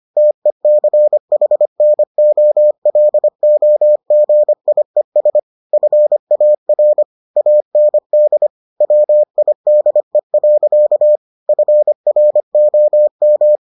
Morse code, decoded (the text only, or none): TECHNOLOGIES FAR AND WIDE. FROM